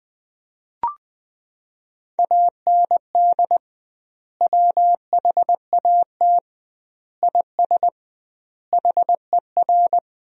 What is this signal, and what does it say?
Morse code, heard as a keyed tone, about 20 wpm, 700 Hz. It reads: AND WHAT IS HER
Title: 2x Word-Spacing - Sentences from Top 200 Words - 20wpm